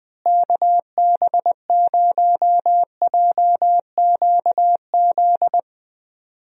Morse code, heard as a keyed tone, about 20 wpm, 700 Hz. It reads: KB0JQZ
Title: USA Call Signs - 6 Characters - Repeated 3x - 20wpm